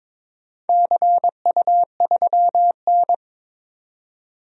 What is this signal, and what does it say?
Morse code, heard as a keyed tone, about 22 wpm, 700 Hz. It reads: CU3N
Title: DXCC-100 Call Signs - 4 Characters - 22wpm